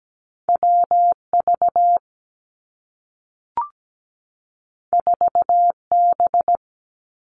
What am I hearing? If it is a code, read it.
Morse code, heard as a keyed tone, about 17 wpm, 700 Hz. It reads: W V 4 B
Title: Two Challenging Letter-Number Combinations 17wpm